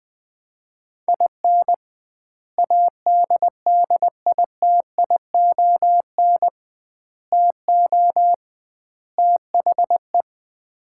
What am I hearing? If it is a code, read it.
Morse code, heard as a keyed tone, about 20 wpm, 700 Hz. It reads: IN ADDITION TO THE
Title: 2x Word-Spacing - Sets of 4 Words - 20wpm